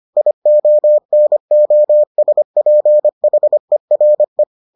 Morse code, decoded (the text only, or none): IONOSPHERE